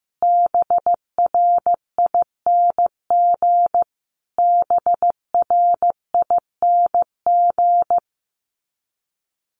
Morse code, heard as a keyed tone, about 15 wpm, 700 Hz. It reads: BRING BRING
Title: Top 500 Words Repeated 15wpm